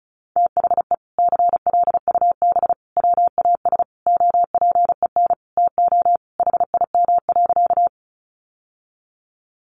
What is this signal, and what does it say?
Morse code, heard as a keyed tone, about 34 wpm, 700 Hz. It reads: THE CLUB WAS OPEN TO HIM.
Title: Sentences from Top 1200 Words 35wpm